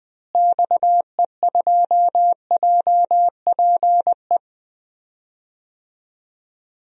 Morse code, heard as a keyed tone, about 21 wpm, 700 Hz. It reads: XE2JPE
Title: DXCC-100 Call Signs - 6 Characters - Repeated 2x - 20wpm